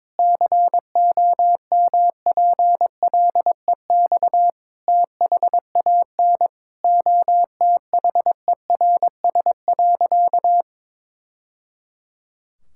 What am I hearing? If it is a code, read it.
Morse code, heard as a keyed tone, about 22 wpm, 700 Hz. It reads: COMPLEX THAN OTHERS.
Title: Sentences from Top 1000 Words - Review Part 6 - 22wpm